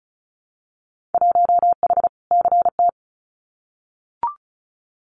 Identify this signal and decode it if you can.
Morse code, heard as a keyed tone, about 34 wpm, 700 Hz. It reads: 1H CT
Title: Winter Field Day Exchanges 35wpm